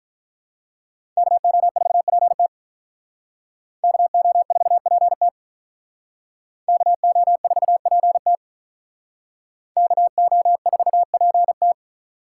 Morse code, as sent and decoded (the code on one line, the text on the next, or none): -.- --- ....- .--. - / -.- --- ....- .--. - / -.- --- ....- .--. - / -.- --- ....- .--. -
KO4PT KO4PT KO4PT KO4PT